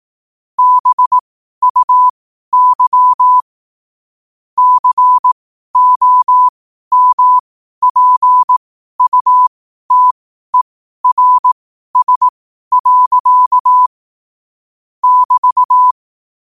Morse code, heard as a keyed tone, about 18 wpm, spaced as for 12 wpm, 1000 Hz. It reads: BUY COMPUTERS. <BT>
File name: qotdfiles / 12wpm